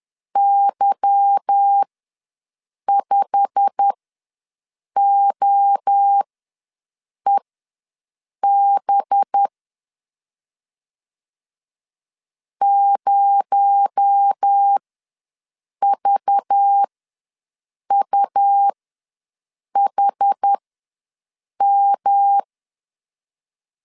Morse code, heard as a keyed tone, about 10 wpm, spaced as for 6 wpm, 800 Hz. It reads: Y5OEB 0VUHM
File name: LEZIONI / 8 YJ45 / 8 YJ45 ws 10wpm 3843